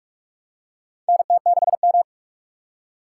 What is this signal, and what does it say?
Morse code, heard as a keyed tone, about 46 wpm, 700 Hz. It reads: NT6M
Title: USA Call Signs - 3x Character Spacing - 45wpm